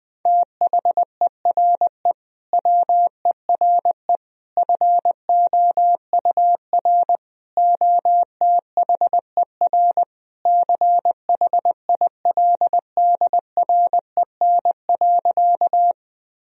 Morse code, heard as a keyed tone, about 21 wpm, 700 Hz. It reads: THERE WERE FOUR OTHER CHILDREN.
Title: Sentences from Top 1000 Words - Review Part 1 - 20wpm